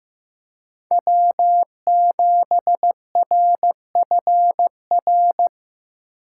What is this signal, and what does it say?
Morse code, heard as a keyed tone, about 15 wpm, 700 Hz. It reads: W7RFR
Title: USA Call Signs - 5 Characters - 8x Character Spacing - 15wpm